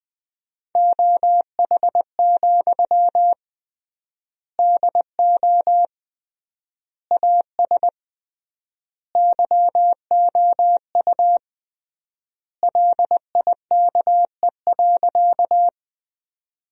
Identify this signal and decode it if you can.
Morse code, heard as a keyed tone, about 20 wpm, 700 Hz. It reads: OH, DO AS YOU LIKE.
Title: 3x Word-Spacing - Sentences from Top 500 Words - 20wpm